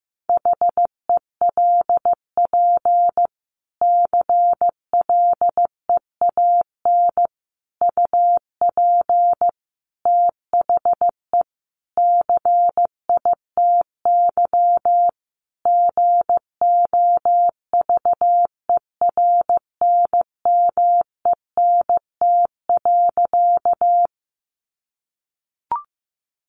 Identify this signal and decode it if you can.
Morse code, heard as a keyed tone, about 15 wpm, 700 Hz. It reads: HELP CLEAN UP THE CITY GOVERNMENT.
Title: Sentences from Top 1100 Words 15wpm